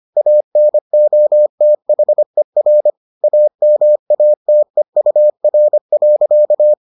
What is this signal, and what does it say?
Morse code, heard as a keyed tone, about 25 wpm, 600 Hz. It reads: ANOTHER AMATEUR.